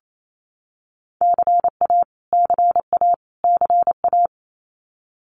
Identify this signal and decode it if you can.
Morse code, heard as a keyed tone, about 28 wpm, 700 Hz. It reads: CA CA CA
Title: US States Repeated 3x 28wpm